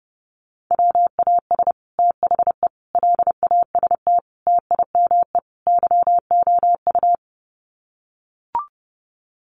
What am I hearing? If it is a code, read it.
Morse code, heard as a keyed tone, about 30 wpm, 700 Hz. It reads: WAS THE LAST TIME YOU
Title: Sets of 5 Words Encore2 30wpm